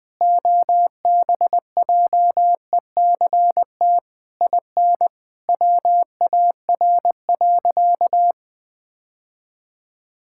Morse code, heard as a keyed tone, about 21 wpm, 700 Hz. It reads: OBJECT IN WAR.
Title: Sentences from Top 1000 Words - Review Part 6 - 20wpm